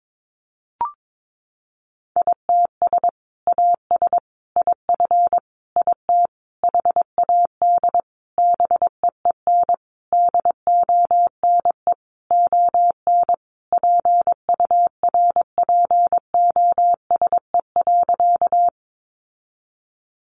Morse code, heard as a keyed tone, about 22 wpm, 700 Hz. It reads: ITS AS IF IT HAD BEEN DONE ON PURPOSE.
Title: Sentences from Top 800 Words 22wpm